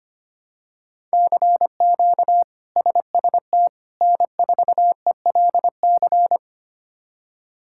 Morse code, heard as a keyed tone, about 25 wpm, 700 Hz. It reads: CQ SST N4ELC
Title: K1USN SST QSO Practice - 25wpm